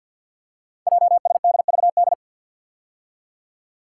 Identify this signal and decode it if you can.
Morse code, heard as a keyed tone, about 52 wpm, 700 Hz. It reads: JSDVD
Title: Code Groups - 5 Letters - 50wpm